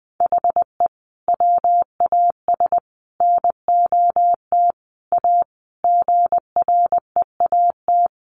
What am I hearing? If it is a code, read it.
Morse code, heard as a keyed tone, about 20 wpm, 700 Hz. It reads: HE WAS NOT A GREAT
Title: Sentences from Top 1200 Words 20wpm